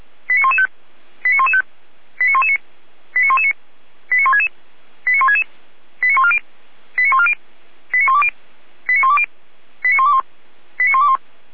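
Five-tone selective call (ZVEI), heard as quick runs of five tones, tones of about 80 ms. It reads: 89115 89115 89119 89119 89166 89166 89140 89140 89120 89120 89121 89121
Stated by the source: ZVEI five-tone call received by SDR, from audio.f85360.feuerwehr-orlamuende.org fire brigade server.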